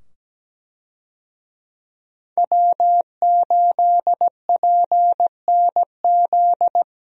Morse code, heard as a keyed tone, about 17 wpm, 700 Hz. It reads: W8PNZ
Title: USA Call Signs - 5 Characters - 17wpm